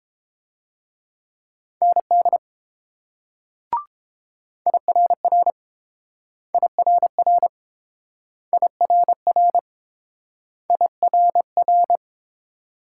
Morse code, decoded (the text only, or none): ND IRR IRR IRR IRR